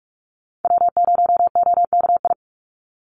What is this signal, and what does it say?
Morse code, heard as a keyed tone, about 44 wpm, 700 Hz. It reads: W0OKI
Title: USA Call Signs 45wpm